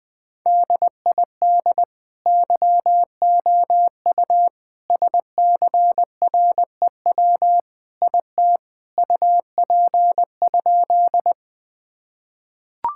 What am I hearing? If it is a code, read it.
Morse code, heard as a keyed tone, about 21 wpm, 700 Hz. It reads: DID YOU SCREW IT UP?
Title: Sentences from Top 1600 Words 20wpm